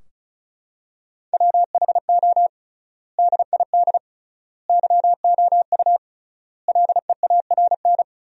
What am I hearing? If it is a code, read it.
Morse code, heard as a keyed tone, about 34 wpm, 700 Hz. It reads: WHO DID YOU LEARN
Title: 3x Word-Spacing - Sentences from Top 300 Words - 35wpm